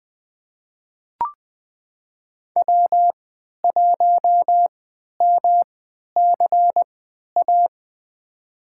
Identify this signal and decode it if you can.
Morse code, heard as a keyed tone, about 20 wpm, spaced as for 11 wpm, 700 Hz. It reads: W1MCA
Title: USA Call Signs - 5 Characters - 3x Character Spacing - 20wpm